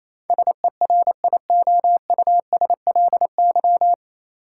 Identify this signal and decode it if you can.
Morse code, heard as a keyed tone, about 28 wpm, 700 Hz. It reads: SERIOUSLY
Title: Sentences from Top 1200 Words - New Word Drill Alt - 28wpm